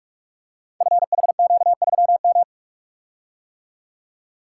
Morse code, heard as a keyed tone, about 44 wpm, 700 Hz. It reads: FHQ3M
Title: Code Groups - 5 Letter:Numbers - 45wpm